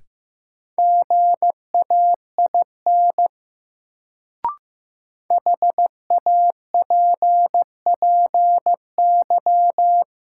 Morse code, heard as a keyed tone, about 15 wpm, 700 Hz. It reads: GAIN HAPPY
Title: Sentences from Top 1000 Words - New Word Drill - 15wpm